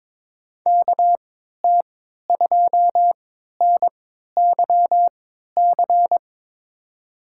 Morse code, heard as a keyed tone, about 23 wpm, spaced as for 13 wpm, 700 Hz. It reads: KT2NYC
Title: USA Call Signs - 6 Characters - 3x Character Spacing - 22wpm